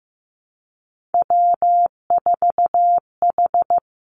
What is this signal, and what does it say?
Morse code, heard as a keyed tone, about 15 wpm, 700 Hz. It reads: W4H
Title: Three Challenging Letter-Number Combinations 15wpm